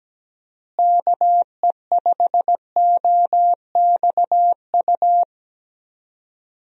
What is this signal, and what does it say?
Morse code, heard as a keyed tone, about 17 wpm, 700 Hz. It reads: KE5OXU
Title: USA Call Signs - 6 Characters - Repeated 3x - 17wpm